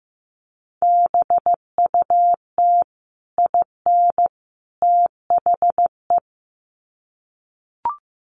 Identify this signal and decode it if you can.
Morse code, heard as a keyed tone, about 15 wpm, 700 Hz. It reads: BUT IN THE
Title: Sets of 3 Words 15wpm